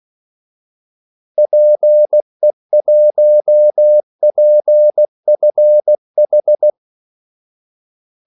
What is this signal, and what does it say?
Morse code, heard as a keyed tone, about 16 wpm, 600 Hz. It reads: PE1PFH
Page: Kaldesignaler 16 wpm